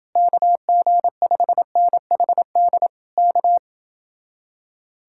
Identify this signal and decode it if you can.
Morse code, heard as a keyed tone, about 29 wpm, 700 Hz. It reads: KG5NHD K